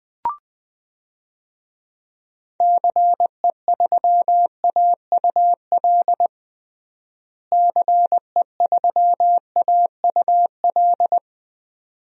Morse code, heard as a keyed tone, about 20 wpm, 700 Hz. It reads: CE3AUL CE3AUL
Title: DXCC-100 Call Signs - 6 Characters - Repeated 2x - 20wpm